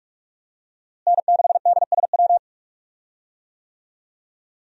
Morse code, heard as a keyed tone, about 44 wpm, 700 Hz. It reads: N6DSW
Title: USA Call Signs - 5 Characters - Repeated 3x - 45wpm